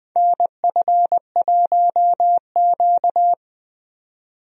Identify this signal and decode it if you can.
Morse code, heard as a keyed tone, about 20 wpm, 700 Hz. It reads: NF1Q
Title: USA Call Signs - 4 Characters - Repeated 3x - 20wpm